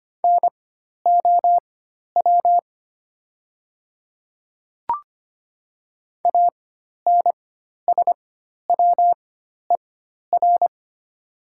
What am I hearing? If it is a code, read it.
Morse code, heard as a keyed tone, about 25 wpm, spaced as for 12 wpm, 700 Hz. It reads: NOW ANSWER
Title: 4x Character Spacing - Top 500 Words - 25wpm